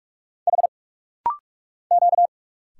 Morse code, heard as a keyed tone, about 46 wpm, 700 Hz. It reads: H Q
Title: Single Letters - Rapid-Fire 45wpm